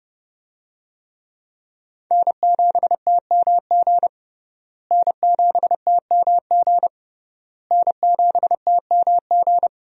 Morse code, heard as a keyed tone, about 30 wpm, 700 Hz. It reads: N7TMG N7TMG N7TMG
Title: USA Call Signs - 5 Characters - Repeated 3x - 30wpm